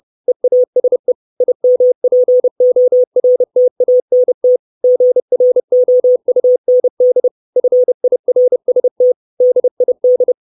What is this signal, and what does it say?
Morse code, heard as a keyed tone, about 30 wpm, 500 Hz. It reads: EASE IMPORTANT GROUND FIRST DID
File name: 632 Words 30WPM